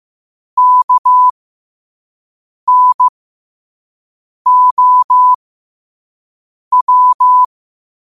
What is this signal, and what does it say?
Morse code, heard as a keyed tone, about 15 wpm, spaced as for 5 wpm, 1000 Hz. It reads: KNOW
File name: qotdfiles / 5wpm